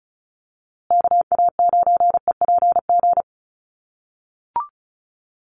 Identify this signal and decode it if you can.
Morse code, heard as a keyed tone, about 34 wpm, 700 Hz. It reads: KA9EPG
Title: USA Call Signs Repeated 3x 35wpm